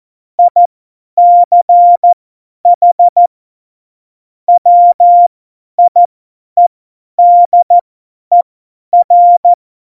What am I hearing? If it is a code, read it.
Morse code, heard as a keyed tone, about 14 wpm, spaced as for 10 wpm, 700 Hz. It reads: ICH WIEDER